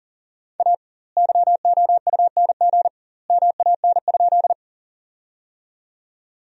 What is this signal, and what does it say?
Morse code, heard as a keyed tone, about 40 wpm, 700 Hz. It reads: A YOUNG MAN?
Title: 2x Word-Spacing - Sentences from Top 400 Words - 40wpm